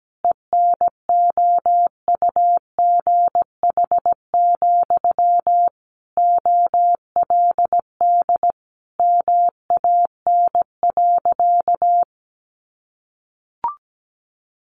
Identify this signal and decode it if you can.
Morse code, heard as a keyed tone, about 17 wpm, 700 Hz. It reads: ENOUGH, OLD MAN.
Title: Sentences from Top 1200 Words 17wpm